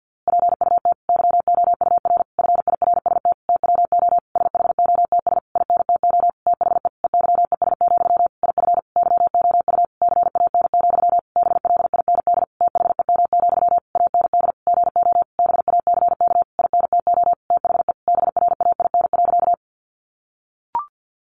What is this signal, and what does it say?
Morse code, heard as a keyed tone, about 50 wpm, 700 Hz. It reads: PUT YOUR FIRST TWO SHOTS INTO THE EYES, IF YOU CAN, BLIND THEM, AND GO BACK INTO THE BRAIN.
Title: A Sound Of Thunder 50wpm